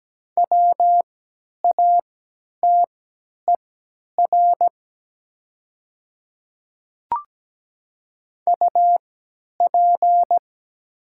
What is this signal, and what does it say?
Morse code, heard as a keyed tone, about 17 wpm, spaced as for 10 wpm, 700 Hz. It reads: WATER UP
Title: 3x Character Spacing - Top 100 Words - 17wpm